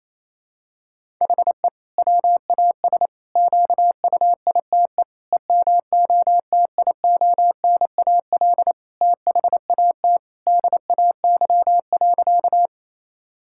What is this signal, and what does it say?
Morse code, heard as a keyed tone, about 28 wpm, 700 Hz. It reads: HE WAS QUITE EMOTIONAL THAT DAY.
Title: Sentences from Top 1000 Words - Review Part 5 - 28wpm